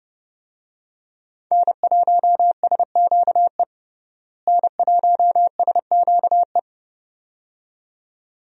Morse code, heard as a keyed tone, about 30 wpm, 700 Hz. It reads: N1SQE N1SQE
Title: USA Call Signs - 5 Characters - Repeated 2x - 30wpm